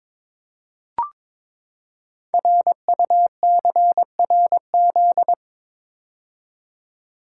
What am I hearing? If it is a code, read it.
Morse code, heard as a keyed tone, about 22 wpm, 700 Hz. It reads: RUCRZ
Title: Code Groups - 5 Letters - 22wpm